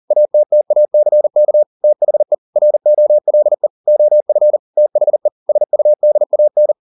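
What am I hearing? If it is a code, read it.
Morse code, heard as a keyed tone, about 40 wpm, 600 Hz. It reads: ATTACK THE ROLE OF THE SUDAN
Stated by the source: Morse code at 40 wpm, practice text, CW med Gnister.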